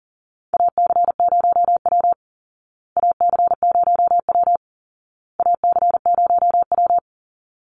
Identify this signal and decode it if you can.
Morse code, heard as a keyed tone, about 40 wpm, 700 Hz. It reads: AC0W AC0W AC0W
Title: Call Signs Repeated 2x - 40wpm